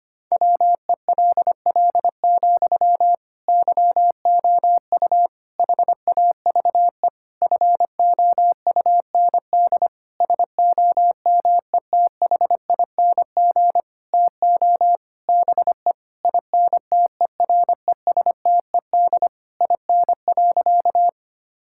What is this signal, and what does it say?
Morse code, heard as a keyed tone, about 25 wpm, 700 Hz. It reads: WELL, YOU HAVE FOUND SOMETHING TO BE INTERESTED IN.
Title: Sentences from Top 1000 Words - Review Part 6 - 25wpm